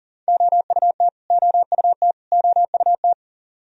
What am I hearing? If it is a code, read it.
Morse code, heard as a keyed tone, about 40 wpm, 700 Hz. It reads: OUT OUT OUT
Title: Top 100 Words Repeated 40wpm